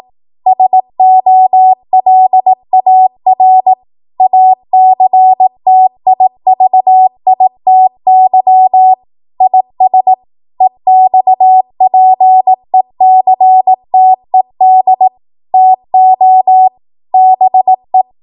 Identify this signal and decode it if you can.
Morse code, heard as a keyed tone, about 18 wpm, 750 Hz. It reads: SOLAR ACTIVITY IS EXPECTED TO BE
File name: Morse / 18WPM